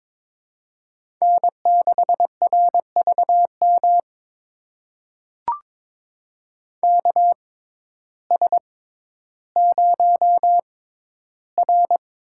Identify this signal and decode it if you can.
Morse code, heard as a keyed tone, about 22 wpm, spaced as for 12 wpm, 700 Hz. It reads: N6RVM KS0R
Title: USA Call Signs - 6x Character Spacing - 22wpm